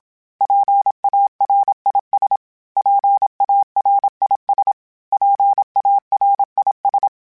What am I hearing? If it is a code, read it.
Morse code, heard as a keyed tone, about 27 wpm, 800 Hz. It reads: PARIS PARIS PARIS
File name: PARIS Calibration / PARIS mp3s / PARIS@30